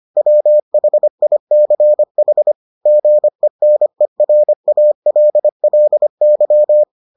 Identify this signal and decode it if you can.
Morse code, heard as a keyed tone, about 25 wpm, 600 Hz. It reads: WHICH GENERALLY